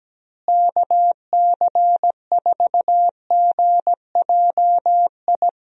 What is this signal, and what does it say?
Morse code, heard as a keyed tone, about 17 wpm, 700 Hz. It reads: KC4GJI
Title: USA Call Signs - 6 Characters - Repeated 2x - 17wpm